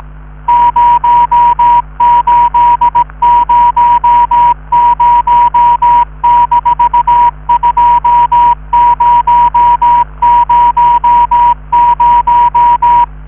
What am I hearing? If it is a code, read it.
Morse code, heard as a keyed tone, about 18 wpm, 960 Hz. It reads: 0800-2000